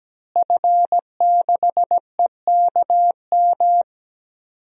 Morse code, heard as a keyed tone, about 17 wpm, 700 Hz. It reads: F6EKM
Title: DXCC-100 Call Signs - 5 Characters - Repeated 2x - 17wpm